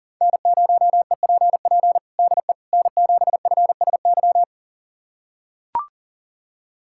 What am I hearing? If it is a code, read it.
Morse code, heard as a keyed tone, about 40 wpm, 700 Hz. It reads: N0EPP DE N7FSY
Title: USA Callsign DE Callsign - 5 Characters - 40wpm